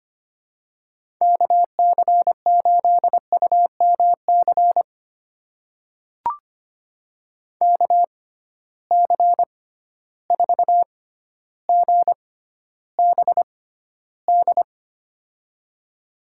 Morse code, heard as a keyed tone, about 25 wpm, spaced as for 13 wpm, 700 Hz. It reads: KC8UMC KC4GBD
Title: USA Call Signs - 6 Characters - 6x Character Spacing - 25wpm